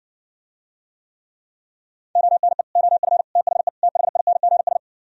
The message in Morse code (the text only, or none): ONE OF THE THINGS